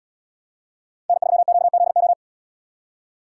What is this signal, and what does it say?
Morse code, heard as a keyed tone, about 77 wpm, 700 Hz. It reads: N3QZO